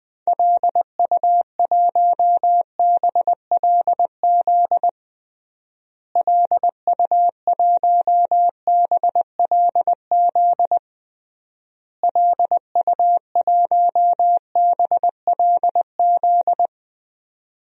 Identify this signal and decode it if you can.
Morse code, heard as a keyed tone, about 19 wpm, 700 Hz. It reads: LU1BLZ LU1BLZ LU1BLZ
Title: DXCC-100 Call Signs - 6 Characters - Repeated 3x - 20wpm